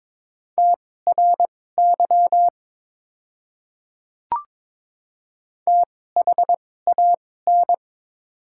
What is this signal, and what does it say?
Morse code, heard as a keyed tone, about 22 wpm, spaced as for 16 wpm, 700 Hz. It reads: TRY THAN
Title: 2x Character Spacing - Top 200 Words - 22wpm